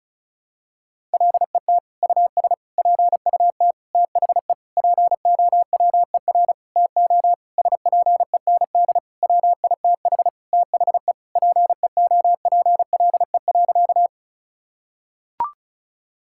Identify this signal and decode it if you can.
Morse code, heard as a keyed tone, about 34 wpm, 700 Hz. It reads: LET US PUT THE POWER TO SPEND WITH THE PEOPLE.
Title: Sentences from Top 1000 Words - Review Part 5 - 35wpm